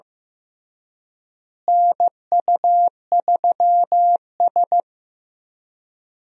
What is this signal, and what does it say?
Morse code, heard as a keyed tone, about 15 wpm, 700 Hz. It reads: NU3S
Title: USA Call Signs - 4 Characters - Repeated 3x - 15wpm